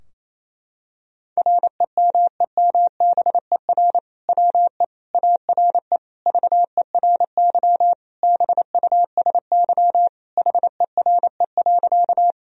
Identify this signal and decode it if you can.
Morse code, heard as a keyed tone, about 28 wpm, 700 Hz. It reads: REMEMBER WE ARE VERY BUSY HERE.
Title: Sentences from Top 1000 Words - Review Part 3 - 28wpm